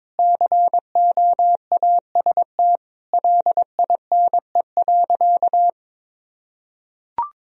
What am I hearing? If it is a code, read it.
Morse code, heard as a keyed tone, about 22 wpm, 700 Hz. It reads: COAST LINE.
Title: Sentences from Top 1000 Words - Review Part 4 - 22wpm